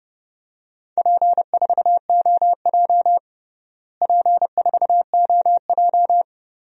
Morse code, heard as a keyed tone, about 30 wpm, 700 Hz. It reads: P4OJ P4OJ
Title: DXCC-100 Call Signs - 4 Characters - Repeated 2x - 30wpm